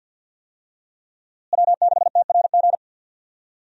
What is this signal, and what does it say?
Morse code, heard as a keyed tone, about 51 wpm, 700 Hz. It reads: W6TRG
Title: USA Call Signs - 5 Characters - Repeated 2x - 50wpm